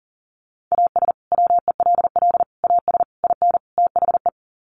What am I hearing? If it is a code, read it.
Morse code, heard as a keyed tone, about 40 wpm, 700 Hz. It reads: AS WELL AS IN THE
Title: Sets of 5 Words Encore 40wpm